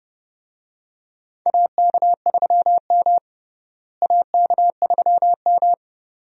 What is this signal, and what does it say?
Morse code, heard as a keyed tone, about 30 wpm, 700 Hz. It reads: AK3M AK3M
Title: USA Call Signs - 4 Characters - Repeated 2x - 30wpm